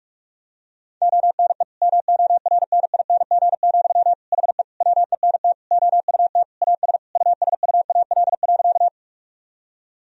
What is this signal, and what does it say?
Morse code, heard as a keyed tone, about 48 wpm, 700 Hz. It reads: ONE MORNING, HE WENT OUT AS USUAL.
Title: Sentences from Top 1000 Words - Review Part 2 - 45wpm